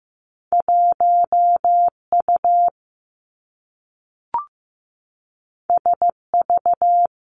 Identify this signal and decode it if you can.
Morse code, heard as a keyed tone, about 15 wpm, 700 Hz. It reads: 1 U S V
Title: Two Challenging Letter-Number Combinations 15wpm